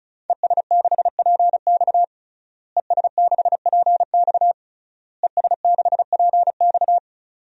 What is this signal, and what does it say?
Morse code, heard as a keyed tone, about 36 wpm, 700 Hz. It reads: ES6PX ES6PX ES6PX
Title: DXCC-100 Call Signs - 5 Characters - Repeated 3x - 35wpm